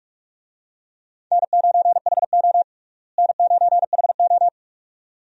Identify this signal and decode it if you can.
Morse code, heard as a keyed tone, about 46 wpm, 700 Hz. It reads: N9HO N9HO
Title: USA Call Signs - 4 Characters - Repeated 2x - 45wpm